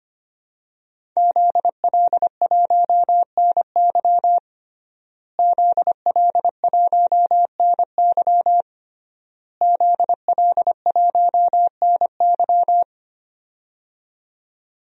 Morse code, heard as a keyed tone, about 25 wpm, 700 Hz. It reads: ZL1NY ZL1NY ZL1NY
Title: DXCC-100 Call Signs - 5 Characters - Repeated 3x - 25wpm